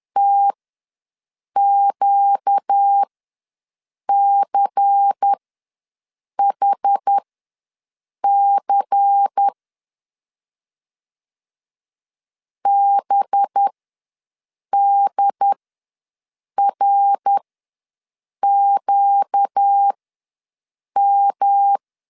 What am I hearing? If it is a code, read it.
Morse code, heard as a keyed tone, about 10 wpm, spaced as for 6 wpm, 800 Hz. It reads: TQCHC BDRQM
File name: LEZIONI / 5 LBVC / 5 LBVC ws 10wpm 3733